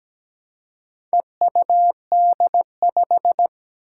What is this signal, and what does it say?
Morse code, heard as a keyed tone, about 17 wpm, 700 Hz. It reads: EUD5
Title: Four Challenging Letter-Number Combinations 17wpm